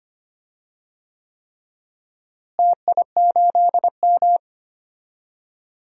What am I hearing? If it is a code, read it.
Morse code, heard as a keyed tone, about 25 wpm, 700 Hz. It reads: TI8M
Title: DXCC-100 Call Signs - 4 Characters - Repeated 3x - 25wpm